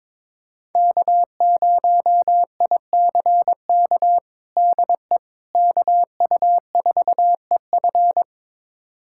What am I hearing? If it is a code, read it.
Morse code, heard as a keyed tone, about 22 wpm, 700 Hz. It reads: K0ICK DE KU4EF
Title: USA Callsign DE Callsign - 5 Characters - 22wpm